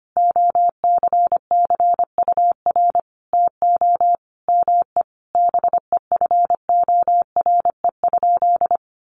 Morse code, heard as a keyed tone, about 25 wpm, 700 Hz. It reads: OCCUR TO ME BEFORE?
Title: Sentences from Top 900 Words 25wpm